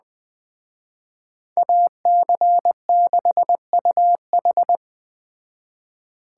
Morse code, heard as a keyed tone, about 21 wpm, 700 Hz. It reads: AC6UH
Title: Speed Racing - USA Call Signs 17wpm